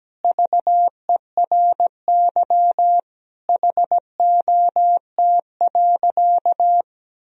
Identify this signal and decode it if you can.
Morse code, heard as a keyed tone, about 17 wpm, 700 Hz. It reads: VERY HOT.
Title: Sentences from Top 1000 Words - Review Part 4 - 17wpm